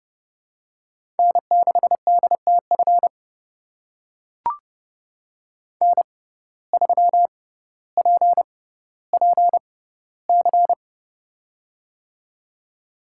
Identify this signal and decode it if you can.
Morse code, heard as a keyed tone, about 30 wpm, spaced as for 15 wpm, 700 Hz. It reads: N6DTF N3PPC
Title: USA Call Signs - 5 Characters - 6x Character Spacing - 30wpm